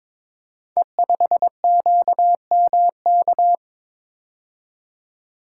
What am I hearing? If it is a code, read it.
Morse code, heard as a keyed tone, about 22 wpm, 700 Hz. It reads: E5QMK
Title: Code Groups - 5 Letter:Numbers - 22wpm